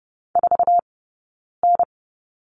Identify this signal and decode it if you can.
Morse code, heard as a keyed tone, about 30 wpm, 700 Hz. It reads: 4 N
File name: Lesson 19d - Letter w - Two Characters 30wpm